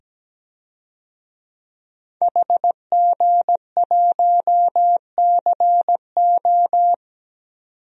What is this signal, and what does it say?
Morse code, heard as a keyed tone, about 17 wpm, 700 Hz. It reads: HG1CO
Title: DXCC-100 Call Signs - 5 Characters - 17wpm